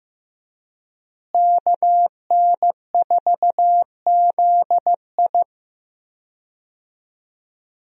Morse code, heard as a keyed tone, about 16 wpm, 700 Hz. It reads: KN4ZI
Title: USA Call Signs - 5 Characters - Repeated 2x - 15wpm